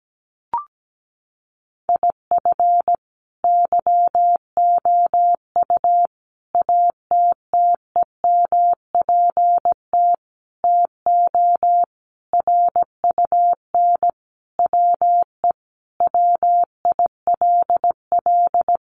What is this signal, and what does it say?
Morse code, heard as a keyed tone, about 17 wpm, 700 Hz. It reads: IF YOU ATTEMPT TO RUN WE WILL
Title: Sentences from Top 1200 Words 17wpm